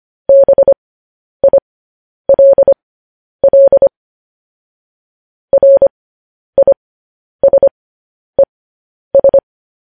Morse code, heard as a keyed tone, about 25 wpm, spaced as for 10 wpm, 550 Hz. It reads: BILL RISES
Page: news-headlines-in-morse-code-at-10-wpm